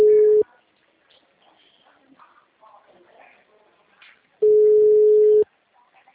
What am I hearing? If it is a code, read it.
Ringback tone (Europe, 425 Hz), with speech.
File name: out- -2025- - - .23770